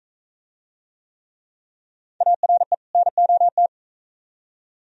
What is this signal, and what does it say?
Morse code, heard as a keyed tone, about 41 wpm, 700 Hz. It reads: ARE NOT